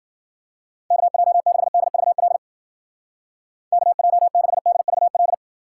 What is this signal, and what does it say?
Morse code, heard as a keyed tone, about 50 wpm, 700 Hz. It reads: KJ6DVL KJ6DVL
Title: Speed Racing - USA Call Signs 40wpm